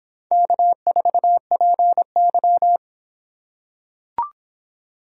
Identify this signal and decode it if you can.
Morse code, heard as a keyed tone, about 26 wpm, 700 Hz. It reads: K4PY